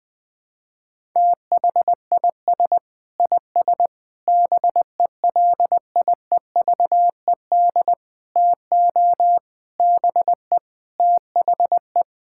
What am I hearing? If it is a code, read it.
Morse code, heard as a keyed tone, about 21 wpm, 700 Hz. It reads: THIS IS BELIEVED TO BE THE
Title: Sentences with 6-Word Phrases - Spoken Beginning - 20wpm